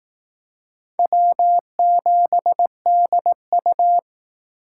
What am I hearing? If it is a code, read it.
Morse code, heard as a keyed tone, about 18 wpm, 700 Hz. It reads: W7DU
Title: Speed Racing - USA Call Signs 15wpm